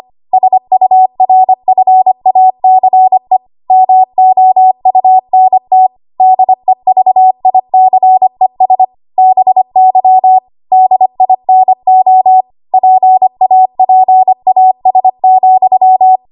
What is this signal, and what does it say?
Morse code, heard as a keyed tone, about 25 wpm, 750 Hz. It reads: SURFACE MOUNT DEVICES BY DINO PAPAS,
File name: Archive / 25 WPM / 25WPM